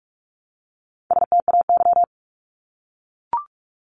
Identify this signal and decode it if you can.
Morse code, heard as a keyed tone, about 44 wpm, 700 Hz. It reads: STAY
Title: Top 500 Words 45wpm